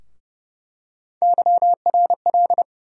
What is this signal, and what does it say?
Morse code, heard as a keyed tone, about 30 wpm, 700 Hz. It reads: YRL
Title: USA Call Sign Postfixes - 3 Letters 30wpm